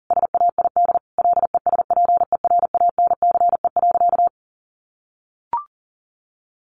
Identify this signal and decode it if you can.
Morse code, heard as a keyed tone, about 40 wpm, 700 Hz. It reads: SAID LESPERANCE.
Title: A Sound Of Thunder 40wpm